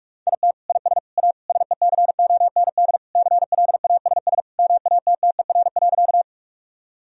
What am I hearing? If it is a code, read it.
Morse code, heard as a keyed tone, about 46 wpm, 700 Hz. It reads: IT IS A SECOND CLASS MATTER.
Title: Sentences from Top 1000 Words - Review Part 2 - 45wpm